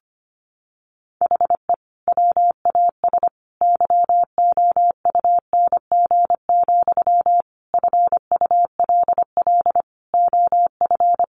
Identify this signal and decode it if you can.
Morse code, heard as a keyed tone, about 25 wpm, 700 Hz. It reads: HE WAS YOUNG, FULL OF
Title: Sentences from Top 1100 Words 25wpm